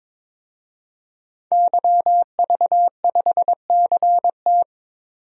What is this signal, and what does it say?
Morse code, heard as a keyed tone, about 22 wpm, 700 Hz. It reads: YV5CT
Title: DXCC-100 Call Signs - 5 Characters - 22wpm